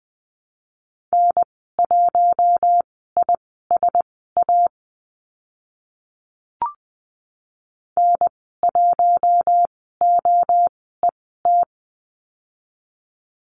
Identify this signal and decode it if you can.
Morse code, heard as a keyed tone, about 20 wpm, spaced as for 14 wpm, 700 Hz. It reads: N1ISA N1OET
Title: 042 Lesson 8g - Number 1 - Call Signs 20wpm